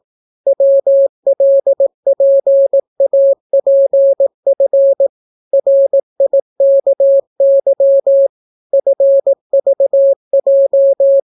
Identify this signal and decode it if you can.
Morse code, heard as a keyed tone, about 18 wpm, 550 Hz. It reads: WLPAPF RIKY FVJ